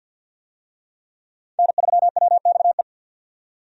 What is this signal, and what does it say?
Morse code, heard as a keyed tone, about 51 wpm, 700 Hz. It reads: N3WXE
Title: USA Call Signs - 5 Characters - 50wpm